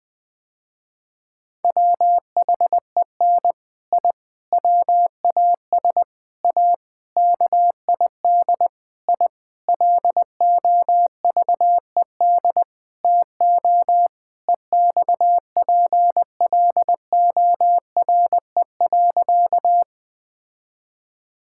Morse code, heard as a keyed tone, about 20 wpm, 700 Hz. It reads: WHEN I WAS A KID I LOVED TO EXPLORE.
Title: Sentences with 6-Word Phrases - Spoken Beginning - 20wpm